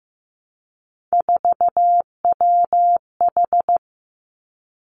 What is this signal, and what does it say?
Morse code, heard as a keyed tone, about 15 wpm, 700 Hz. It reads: 4WH
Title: Three Challenging Letter-Number Combinations 15wpm